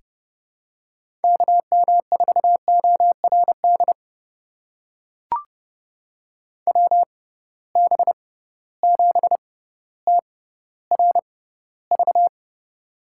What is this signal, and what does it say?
Morse code, heard as a keyed tone, about 31 wpm, spaced as for 16 wpm, 700 Hz. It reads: KM4ORD WB7TRV
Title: USA Call Signs - 6 Characters - 6x Character Spacing - 30wpm